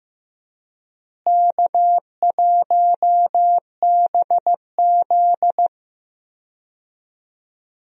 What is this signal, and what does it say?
Morse code, heard as a keyed tone, about 15 wpm, 700 Hz. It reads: K1BZ
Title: USA Call Signs - 4 Characters - Repeated 2x - 15wpm